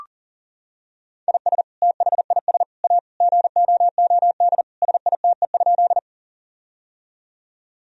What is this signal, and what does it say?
Morse code, heard as a keyed tone, about 40 wpm, 700 Hz. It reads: IS THIS A GOOD SITE?
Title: Sentences from Top 1000 Words - Review Part 4 - 40wpm